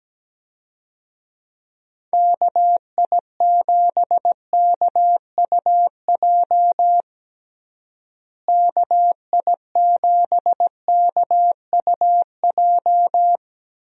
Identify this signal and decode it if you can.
Morse code, heard as a keyed tone, about 17 wpm, 700 Hz. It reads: KI7KUJ KI7KUJ
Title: USA Call Signs - 6 Characters - Repeated 3x - 17wpm